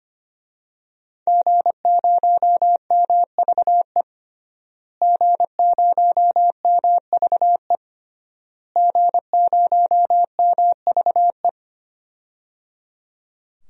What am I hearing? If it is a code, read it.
Morse code, heard as a keyed tone, about 25 wpm, 700 Hz. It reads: G0MVE G0MVE G0MVE
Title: DXCC-100 Call Signs - 5 Characters - Repeated 3x - 25wpm